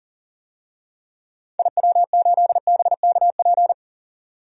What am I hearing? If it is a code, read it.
Morse code, heard as a keyed tone, about 40 wpm, 700 Hz. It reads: IW8BKP
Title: DXCC-100 Call Signs - 6 Characters - 40wpm